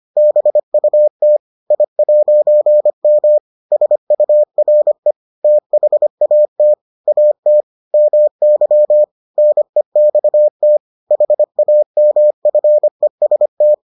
Morse code, heard as a keyed tone, about 25 wpm, 600 Hz. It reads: BUT I'M SURE THAT AT MY NEXT HAMFEST